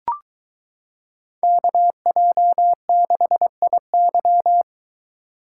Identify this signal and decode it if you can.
Morse code, heard as a keyed tone, about 22 wpm, 700 Hz. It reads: KJ6IY